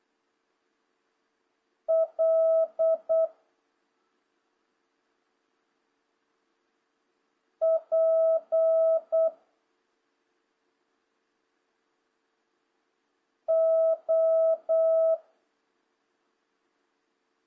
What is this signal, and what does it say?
Morse code, heard as a keyed tone, about 8 wpm, 650 Hz. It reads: L P O
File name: jh8cbh / 2026cw